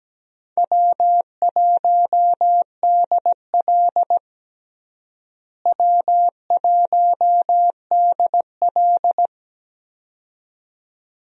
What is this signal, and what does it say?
Morse code, heard as a keyed tone, about 17 wpm, 700 Hz. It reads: W1DL W1DL
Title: USA Call Signs - 4 Characters - Repeated 2x - 17wpm